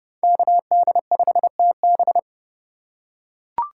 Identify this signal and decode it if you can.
Morse code, heard as a keyed tone, about 30 wpm, 700 Hz. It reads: KD5TB
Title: USA Call Signs - 5 Characters - 2x Character Spacing - 30wpm